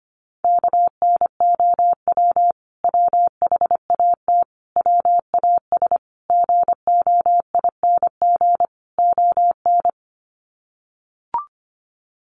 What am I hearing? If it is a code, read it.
Morse code, heard as a keyed tone, about 25 wpm, 700 Hz. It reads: KNOW WHAT WAS GOING ON
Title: Sets of 5 Words Encore2 25wpm